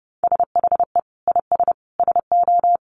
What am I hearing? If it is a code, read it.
Morse code, heard as a keyed tone, about 30 wpm, 700 Hz. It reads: SHE IS SO
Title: Sentences from Top 1500 Words 30wpm